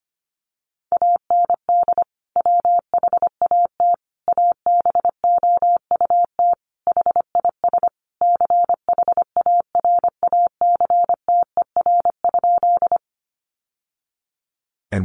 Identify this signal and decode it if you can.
Morse code, heard as a keyed tone, about 25 wpm, 700 Hz. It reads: AND WHAT ABOUT HIS CHARACTER?
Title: Sentences from Top 1200 Words 25wpm